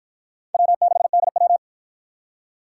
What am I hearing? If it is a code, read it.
Morse code, heard as a keyed tone, about 50 wpm, 700 Hz. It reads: W6DW